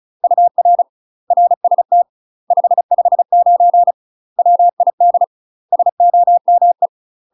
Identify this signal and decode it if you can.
Morse code, heard as a keyed tone, about 34 wpm, 700 Hz. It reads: UR RST 559 WID SOME
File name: UK MASTER QSOs TRAINING / 35WPM / UK MASTER QSOs 0024